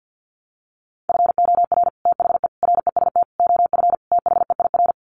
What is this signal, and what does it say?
Morse code, heard as a keyed tone, about 50 wpm, 700 Hz. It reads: FOR THE REST OF THEIR
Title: Sets of 5 Words Encore2 50wpm